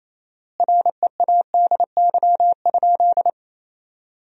Morse code, heard as a keyed tone, about 28 wpm, 700 Hz. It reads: READY?
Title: 4x Word-Spacing - Sentences from Top 400 Words - 28wpm